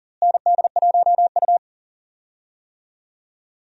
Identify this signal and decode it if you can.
Morse code, heard as a keyed tone, about 40 wpm, 700 Hz. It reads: ND1U
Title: USA Call Signs - 4 Characters - Repeated 2x - 40wpm